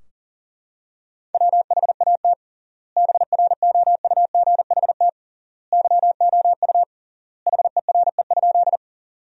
Morse code, heard as a keyed tone, about 40 wpm, 700 Hz. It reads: WHAT BROUGHT YOU HERE?
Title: 3x Word-Spacing - Sentences from Top 500 Words - 40wpm